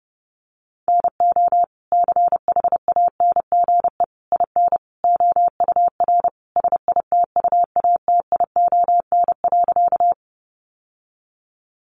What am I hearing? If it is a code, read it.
Morse code, heard as a keyed tone, about 28 wpm, 700 Hz. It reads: NO CHANGE IN OUR SITUATION.
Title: Sentences from Top 1200 Words 30wpm